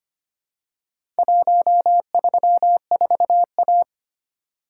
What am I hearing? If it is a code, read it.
Morse code, heard as a keyed tone, about 25 wpm, 700 Hz. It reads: 134A
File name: practice / 25wpm / Four Letter-Number Combinations - 25wpm